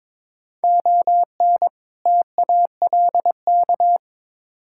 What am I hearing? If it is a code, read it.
Morse code, heard as a keyed tone, about 22 wpm, 700 Hz. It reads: ON TALK
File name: practice / 22wpm / Build A Phrase - 5-Word Phrases - 22wpm